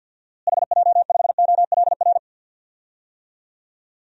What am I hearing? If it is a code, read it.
Morse code, heard as a keyed tone, about 52 wpm, 700 Hz. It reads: HJ5OLR
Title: DXCC-100 Call Signs - 6 Characters - Repeated 3x - 50wpm